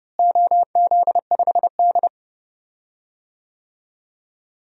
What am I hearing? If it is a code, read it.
Morse code, heard as a keyed tone, about 30 wpm, 700 Hz. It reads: OZ5D
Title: DXCC-100 Call Signs - 4 Characters - Repeated 2x - 30wpm